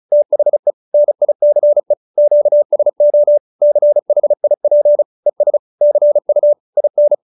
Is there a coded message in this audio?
Yes: Morse code from 0.1 to 7.1 s, heard as a keyed tone, about 35 wpm, 590 Hz. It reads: THE NICE QSO CHIP ES CU IN